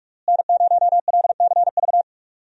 Morse code, heard as a keyed tone, about 42 wpm, 700 Hz. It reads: N0LCV
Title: USA Call Signs - 5 Characters - Repeated 3x - 45wpm